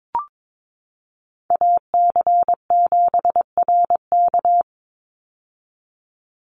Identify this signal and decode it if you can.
Morse code, heard as a keyed tone, about 22 wpm, 700 Hz. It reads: AC7RK
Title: USA Call Signs 22wpm